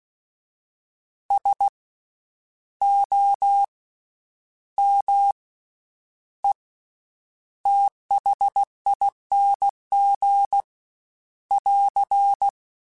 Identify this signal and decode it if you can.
Morse code, heard as a keyed tone, about 16 wpm, spaced as for 9 wpm, 800 Hz. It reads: SOMETHING <AR>
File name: Special INGs-mp3s / Special INGs-mp3s / Slow ing @15